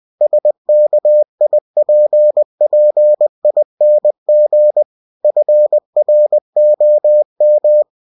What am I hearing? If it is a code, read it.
Morse code, heard as a keyed tone, about 20 wpm, 600 Hz. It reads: SKIPPING FROM
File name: cw-books / winnie-the-pooh / winnie-20wpm-page-0028